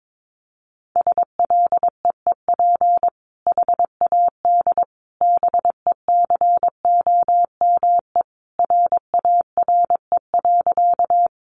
Morse code, heard as a keyed tone, about 22 wpm, 700 Hz. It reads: SLEEP HAD BECOME RARE.
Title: Sentences from Top 1100 Words 22wpm